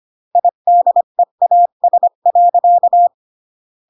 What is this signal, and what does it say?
Morse code, heard as a keyed tone, about 25 wpm, 700 Hz. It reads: IDEAS.